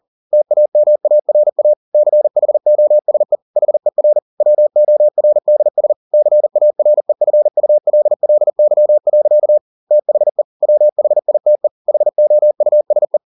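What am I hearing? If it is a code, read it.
Morse code, heard as a keyed tone, about 40 wpm, 600 Hz. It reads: TAMARA CHOSE HER WORDS CAREFULLY. THE WHITE HOUSE
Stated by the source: Morse code at 40 wpm, practice text, CW med Gnister.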